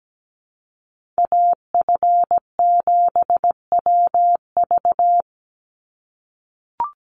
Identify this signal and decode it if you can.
Morse code, heard as a keyed tone, about 17 wpm, 700 Hz. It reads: AF7WV
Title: USA Call Signs 17wpm